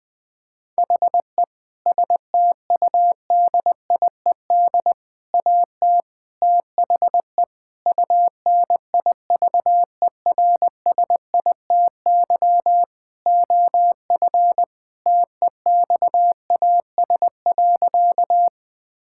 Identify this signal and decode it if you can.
Morse code, heard as a keyed tone, about 20 wpm, 700 Hz. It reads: HE STUDIED AT THE UNIVERSITY OF TEXAS.
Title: Sentences with 4-Word Phrases - 20wpm